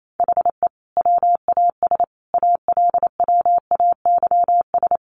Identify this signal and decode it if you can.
Morse code, heard as a keyed tone, about 28 wpm, 700 Hz. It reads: HE WAS ALWAYS
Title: Sentences from Top 1300 Words 28wpm